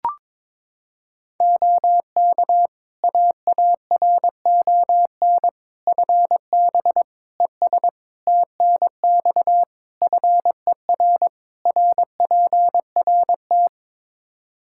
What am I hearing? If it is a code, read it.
Morse code, heard as a keyed tone, about 22 wpm, 700 Hz. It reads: OK AARON FB ES TNX FER RPRT
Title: K1USN SST QSO Practice 22wpm